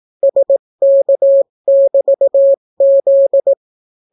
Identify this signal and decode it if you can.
Morse code, heard as a keyed tone, about 18 wpm, 550 Hz. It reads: SK<BT>Z